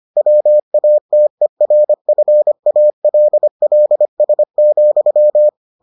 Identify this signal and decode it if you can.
Morse code, heard as a keyed tone, about 25 wpm, 600 Hz. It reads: WATERFALLS,